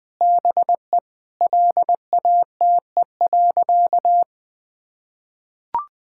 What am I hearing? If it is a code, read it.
Morse code, heard as a keyed tone, about 21 wpm, 700 Hz. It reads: BE LATE.
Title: Sentences with 5-Word Phrases - 20wpm